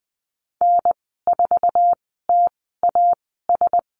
Morse code, heard as a keyed tone, about 20 wpm, spaced as for 15 wpm, 700 Hz. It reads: N4TAS
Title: 048 Lesson 9g - Number 4 - Call Signs 20wpm